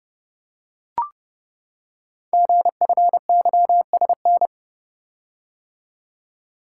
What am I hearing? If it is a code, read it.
Morse code, heard as a keyed tone, about 28 wpm, 700 Hz. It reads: GFYSN
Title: Code Groups - 5 Letter:Numbers - 30wpm